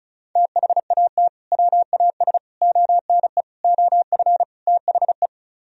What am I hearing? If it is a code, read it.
Morse code, heard as a keyed tone, about 43 wpm, 700 Hz. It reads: THAT WAS ONE OF THE
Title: Speed-Racing - Sets of 5 Words 35wpm